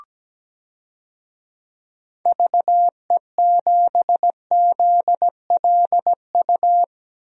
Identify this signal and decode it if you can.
Morse code, heard as a keyed tone, about 17 wpm, 700 Hz. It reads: VE7ZLU
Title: DXCC-100 Call Signs - 6 Characters - Repeated 3x - 17wpm